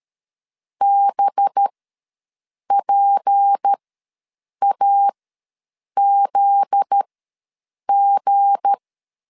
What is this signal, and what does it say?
Morse code, heard as a keyed tone, about 13 wpm, spaced as for 7 wpm, 800 Hz. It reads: BPAZG
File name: LEZIONI / 10 SIGNS / 10B INT ws 12wpm 3527